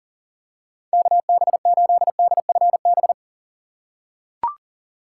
Morse code, heard as a keyed tone, about 37 wpm, 700 Hz. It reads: KB8DFB
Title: USA Call Signs - 6 Characters - 4x Character Spacing - 40wpm